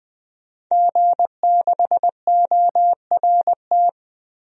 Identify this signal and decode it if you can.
Morse code, heard as a keyed tone, about 20 wpm, 700 Hz. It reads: G6ORT
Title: DXCC-100 Call Signs - 5 Characters - 20wpm